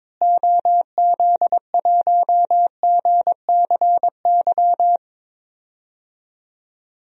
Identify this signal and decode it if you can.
Morse code, heard as a keyed tone, about 22 wpm, 700 Hz. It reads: OZ1GCY
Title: DXCC-100 Call Signs - 6 Characters - Repeated 3x - 22wpm